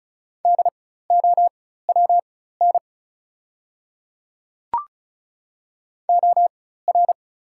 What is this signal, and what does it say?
Morse code, heard as a keyed tone, about 34 wpm, spaced as for 16 wpm, 690 Hz. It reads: DOWN OR
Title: 4x Character Spacing - Top 100 Words - 35wpm